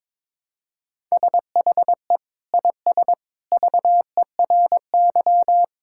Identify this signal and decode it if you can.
Morse code, heard as a keyed tone, about 21 wpm, 700 Hz. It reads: SHE IS VERY
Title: Sentences from Top 1000 Words - Review Part 6 - 22wpm